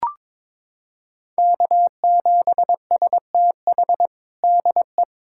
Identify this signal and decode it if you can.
Morse code, heard as a keyed tone, about 22 wpm, 700 Hz. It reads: K7STH DE
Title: USA Callsign DE Callsign - 5 Characters - 22wpm